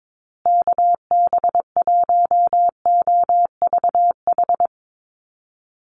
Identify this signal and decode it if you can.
Morse code, heard as a keyed tone, about 22 wpm, 700 Hz. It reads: KB1OVH
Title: USA Call Signs Repeated 2x 22wpm